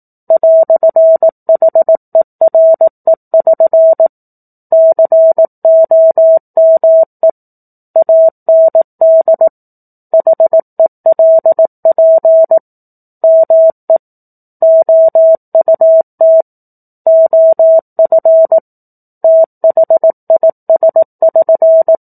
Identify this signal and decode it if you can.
Morse code, heard as a keyed tone, about 18 wpm, 650 Hz. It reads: "HERE<SN> COME AND HELP ME OUT OF THIS<SN>
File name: 04-Chapter-04 18 WPM